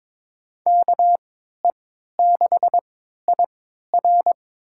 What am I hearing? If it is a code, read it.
Morse code, heard as a keyed tone, about 23 wpm, spaced as for 13 wpm, 700 Hz. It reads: KE6IR
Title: USA Call Signs - 5 Characters - 3x Character Spacing - 22wpm